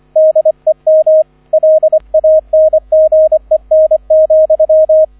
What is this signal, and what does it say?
Morse code, heard as a keyed tone, about 25 wpm, 630 Hz, with radio noise. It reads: DEM LANGEN,